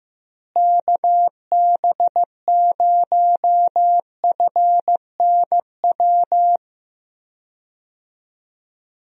Morse code, heard as a keyed tone, about 15 wpm, 700 Hz. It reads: KB0FNW
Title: USA Call Signs - 6 Characters - Repeated 2x - 15wpm